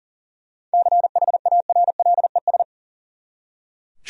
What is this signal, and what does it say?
Morse code, heard as a keyed tone, about 40 wpm, 700 Hz. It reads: CHARLES
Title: USA Common Names - 40wpm